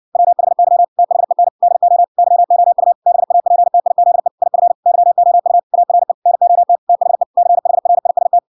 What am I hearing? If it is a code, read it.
Morse code, heard as a keyed tone, about 57 wpm, 700 Hz. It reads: WHY THEN DO YOU BAPTIZE IF YOU ARE NOT THE CHRIST